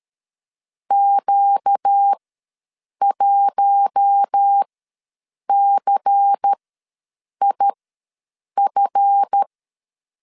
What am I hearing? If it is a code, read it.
Morse code, heard as a keyed tone, about 13 wpm, spaced as for 7 wpm, 800 Hz. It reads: Q1CIF
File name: LEZIONI / 7 GF9Z / 7 GF ws 12wpm 3140